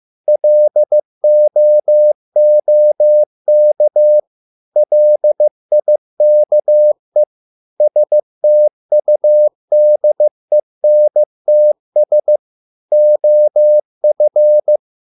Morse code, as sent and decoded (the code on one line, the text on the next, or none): .-.. --- --- -.- / .-.. .. -.- . / ... - ..- -.. -. - ... / --- ..-.
LOOK LIKE STUDENTS OF